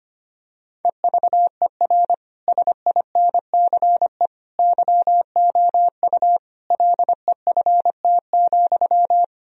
Morse code, heard as a keyed tone, about 25 wpm, 700 Hz. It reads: EVER SINCE YOU LEFT,
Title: Sentences from Top 1000 Words - Review Part 1 - 25wpm